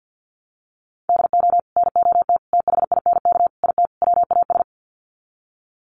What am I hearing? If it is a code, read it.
Morse code, heard as a keyed tone, about 50 wpm, 700 Hz. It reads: DO NOT THINK IT WAS
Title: Sets of 5 Words Encore 50wpm